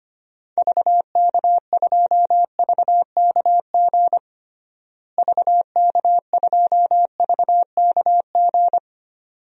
Morse code, heard as a keyed tone, about 25 wpm, 700 Hz. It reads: VK2VKG VK2VKG
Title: DXCC-100 Call Signs - 6 Characters - Repeated 2x - 25wpm